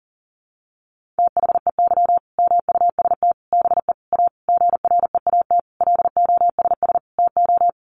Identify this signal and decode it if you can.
Morse code, heard as a keyed tone, about 40 wpm, 700 Hz. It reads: THEY MUST BE A GREAT LOSS TO
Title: Sentences from Top 1400 Words 40wpm